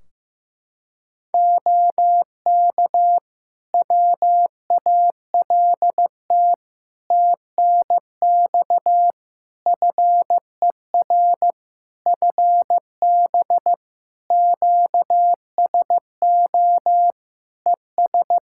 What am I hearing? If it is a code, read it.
Morse code, heard as a keyed tone, about 15 wpm, 700 Hz. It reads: OK WALT TNX FER FB QSO ES
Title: Practice QSO - Exchange 4 - Ending - 15wpm